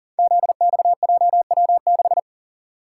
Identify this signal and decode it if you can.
Morse code, heard as a keyed tone, about 40 wpm, 700 Hz. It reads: ZXJW6
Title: Code Groups - 5 Letter:Numbers - 40wpm